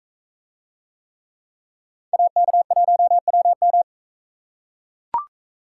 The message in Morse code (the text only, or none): AK1WM